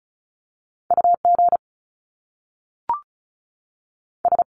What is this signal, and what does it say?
Morse code, heard as a keyed tone, about 36 wpm, spaced as for 6 wpm, 700 Hz. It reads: UGS
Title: Top Two Letter Combinations in Words 35wpm